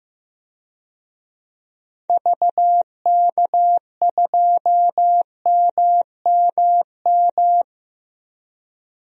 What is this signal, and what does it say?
Morse code, heard as a keyed tone, about 15 wpm, 700 Hz. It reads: VK2MMM
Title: DXCC-100 Call Signs - 6 Characters - 15wpm